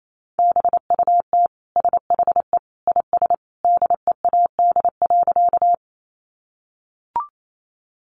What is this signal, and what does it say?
Morse code, heard as a keyed tone, about 28 wpm, 700 Hz. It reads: BUT SHE IS DEAD.
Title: Sentences from Top 1500 Words 28wpm